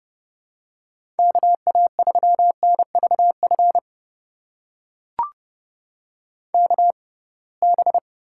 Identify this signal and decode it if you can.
Morse code, heard as a keyed tone, about 30 wpm, spaced as for 23 wpm, 700 Hz. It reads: KA3NVF KB
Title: USA Call Signs - 6 Characters - 6x Character Spacing - 30wpm